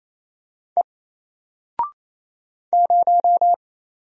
Morse code, heard as a keyed tone, about 27 wpm, 700 Hz. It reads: E 0
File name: Single-Letter-Number-28wpm